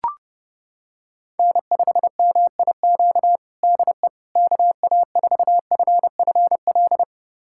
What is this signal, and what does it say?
Morse code, heard as a keyed tone, about 30 wpm, 700 Hz. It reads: N5MIQ DE KA4FFL
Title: USA Callsign DE Callsign 30wpm